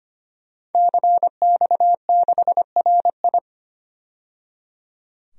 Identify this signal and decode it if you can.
Morse code, heard as a keyed tone, about 25 wpm, 700 Hz. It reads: CX6RI
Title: DXCC-100 Call Signs - 5 Characters - 25wpm